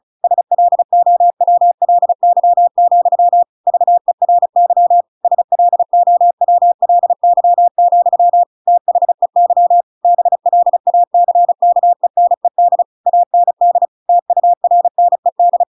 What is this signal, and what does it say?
Morse code, heard as a keyed tone, about 34 wpm, 700 Hz. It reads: SLOWLY, VERY SLOWLY, THEY BLACKENED AND TURNED